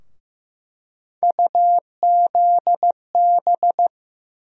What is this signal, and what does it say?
Morse code, heard as a keyed tone, about 15 wpm, 700 Hz. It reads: UZB
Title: USA Call Sign Postfixes - 3 Letters 15wpm